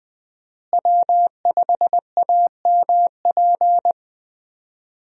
Morse code, heard as a keyed tone, about 20 wpm, 700 Hz. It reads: W5AMP
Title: USA Call Signs - 5 Characters - Repeated 2x - 20wpm